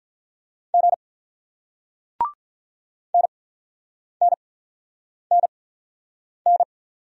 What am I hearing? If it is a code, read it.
Morse code, heard as a keyed tone, about 47 wpm, spaced as for 9 wpm, 690 Hz. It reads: G NNNN